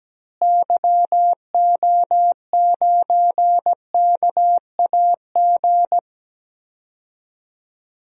Morse code, heard as a keyed tone, about 17 wpm, 700 Hz. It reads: YO9KAG
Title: DXCC-100 Call Signs - 6 Characters - Repeated 3x - 17wpm